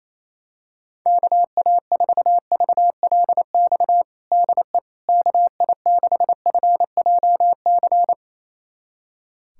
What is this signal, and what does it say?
Morse code, heard as a keyed tone, about 27 wpm, 700 Hz. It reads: KA4VLX DE KI6FJC
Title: USA Callsign DE Callsign - 6 Characters - 28wpm